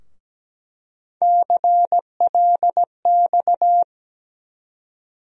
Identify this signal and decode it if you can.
Morse code, heard as a keyed tone, about 17 wpm, 700 Hz. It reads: CLX
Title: USA Call Sign Postfixes - 3 Letters 17wpm